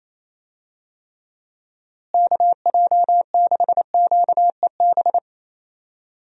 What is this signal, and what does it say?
Morse code, heard as a keyed tone, about 27 wpm, 700 Hz. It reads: KJ6QEB
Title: USA Call Signs - 6 Characters - Repeated 2x - 28wpm